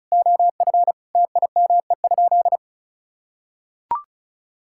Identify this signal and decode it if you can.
Morse code, heard as a keyed tone, about 36 wpm, 700 Hz. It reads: OF TIME?
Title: Sentences from Top 1000 Words - Review Part 5 - 35wpm